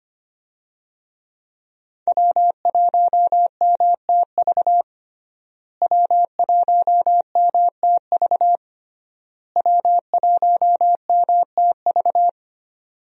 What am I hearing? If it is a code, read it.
Morse code, heard as a keyed tone, about 25 wpm, 700 Hz. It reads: W1MTV W1MTV W1MTV
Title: USA Call Signs - 5 Characters - Repeated 3x - 25wpm